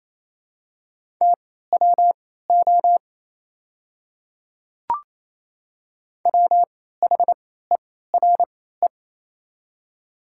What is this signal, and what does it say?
Morse code, heard as a keyed tone, about 28 wpm, spaced as for 16 wpm, 700 Hz. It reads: TWO WHERE
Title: 3x Character Spacing - Top 200 Words - 28wpm